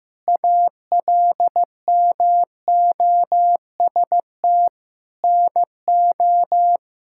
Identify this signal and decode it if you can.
Morse code, heard as a keyed tone, about 15 wpm, 700 Hz. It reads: ALMOST NO
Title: Sentences from Top 1000 Words - Review Part 6 - 15wpm